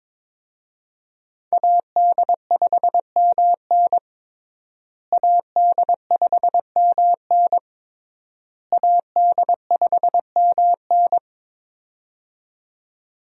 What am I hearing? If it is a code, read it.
Morse code, heard as a keyed tone, about 22 wpm, 700 Hz. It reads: AD5MN AD5MN AD5MN
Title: USA Call Signs - 5 Characters - Repeated 3x - 22wpm